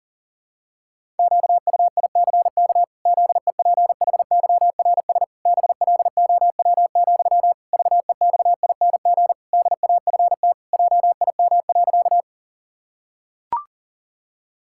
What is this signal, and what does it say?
Morse code, heard as a keyed tone, about 40 wpm, 700 Hz. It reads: QUICK ZEPHYRS BLOW, VEXING DAFT JIM.
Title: Pangram Challenge v2 - 40wpm